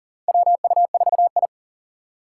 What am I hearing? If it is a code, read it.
Morse code, heard as a keyed tone, about 40 wpm, 700 Hz. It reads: WU4I
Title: USA Call Signs - 2x Character Spacing - 40wpm